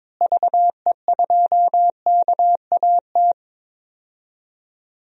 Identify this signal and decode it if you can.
Morse code, heard as a keyed tone, about 22 wpm, 700 Hz. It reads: VE2KAT
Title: DXCC-100 Call Signs - 6 Characters - Repeated 2x - 22wpm